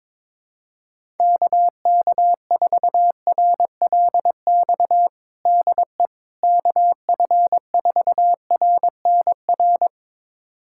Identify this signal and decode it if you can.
Morse code, heard as a keyed tone, about 22 wpm, 700 Hz. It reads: KK4RLX DE KF4RNR
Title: USA Callsign DE Callsign - 6 Characters - 22wpm